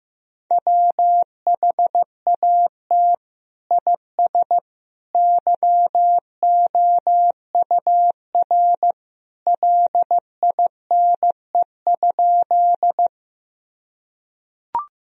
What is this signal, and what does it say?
Morse code, heard as a keyed tone, about 16 wpm, 700 Hz. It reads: WHAT IS YOUR LINE?
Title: Sentences from Top 1000 Words - Review Part 1 - 15wpm